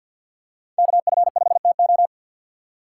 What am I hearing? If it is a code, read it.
Morse code, heard as a keyed tone, about 47 wpm, 700 Hz. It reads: KF5TO
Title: USA Call Signs - 5 Characters - 3x Character Spacing - 50wpm